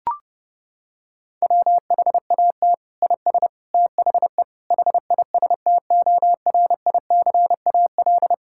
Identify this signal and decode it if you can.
Morse code, heard as a keyed tone, about 30 wpm, 700 Hz. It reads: WHAT IS THE HISTORICAL
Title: Sentences from Top 1000 Words - Review Part 5 - 30wpm